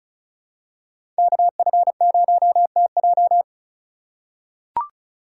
Morse code, heard as a keyed tone, about 34 wpm, 700 Hz. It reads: KF0TJ
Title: USA Call Signs - 5 Characters - 4x Character Spacing - 35wpm